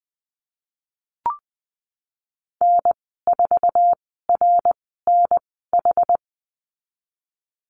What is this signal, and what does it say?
Morse code, heard as a keyed tone, about 20 wpm, spaced as for 14 wpm, 700 Hz. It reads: N4RNH
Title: 062 Lesson 11g - Letter h - Call Signs 20wpm